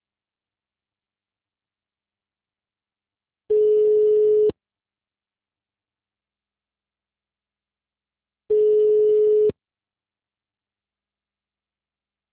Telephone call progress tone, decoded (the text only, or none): ringback tone